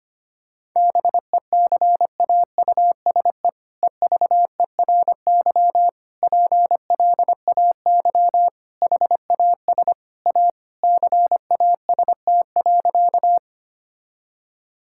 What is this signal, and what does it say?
Morse code, heard as a keyed tone, about 25 wpm, 700 Hz. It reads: BECAUSE EVERY PLAY HAS A CAST.
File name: practice / 25wpm / All-Age Appropriate Jokes - 25wpm